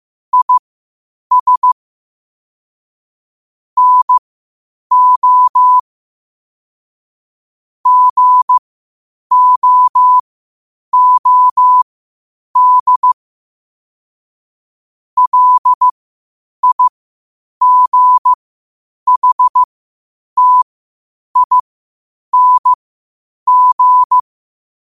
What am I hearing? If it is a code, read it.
Morse code, heard as a keyed tone, about 15 wpm, spaced as for 8 wpm, 1000 Hz. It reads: IS NO GOOD LIGHTING
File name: qotdfiles / 8wpm